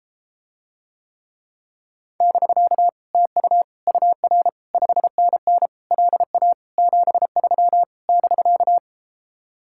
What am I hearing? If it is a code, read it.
Morse code, heard as a keyed tone, about 33 wpm, 700 Hz. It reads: <BK> TU UR 5NN LA 73 <BK>